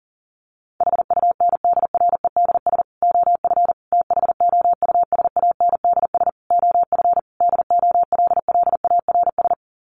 Morse code, heard as a keyed tone, about 40 wpm, 700 Hz. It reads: HUNDREDS OF THOUSANDS OF DOLLARS
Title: Sets of 5 Words Encore 40wpm